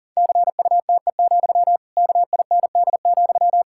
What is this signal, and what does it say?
Morse code, heard as a keyed tone, about 40 wpm, 700 Hz. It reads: CUTE, KIND,
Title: Pangram Challenge v2 - 40wpm